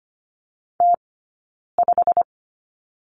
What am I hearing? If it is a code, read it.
Morse code, heard as a keyed tone, about 25 wpm, spaced as for 9 wpm, 690 Hz. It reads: T5
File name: Lesson 15d - Number 5 - Two Characters 25wpm